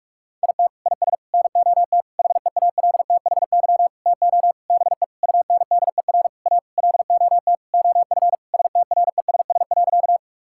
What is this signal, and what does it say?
Morse code, heard as a keyed tone, about 46 wpm, 700 Hz. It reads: IT IS NOT HEALTHY TO BE UNDER A LOT OF STRESS.
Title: Sentences from Top 1000 Words - Review Part 6 - 45wpm